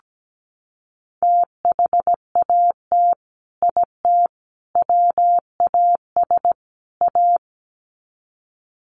Morse code, heard as a keyed tone, about 17 wpm, 700 Hz. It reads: THAT IT WAS A
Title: Sets of 4 Words 17wpm